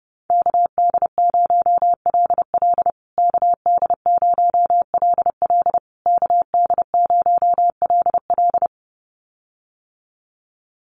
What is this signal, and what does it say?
Morse code, heard as a keyed tone, about 28 wpm, 700 Hz. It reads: KD0LL KD0LL KD0LL
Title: USA Call Signs Repeated 3x 30wpm